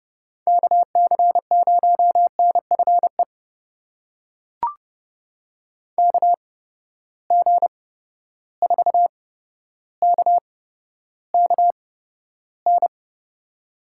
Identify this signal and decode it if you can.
Morse code, heard as a keyed tone, about 31 wpm, spaced as for 13 wpm, 700 Hz. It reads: KC0NFE KG4KKN
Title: USA Call Signs - 6 Characters - 8x Character Spacing - 30wpm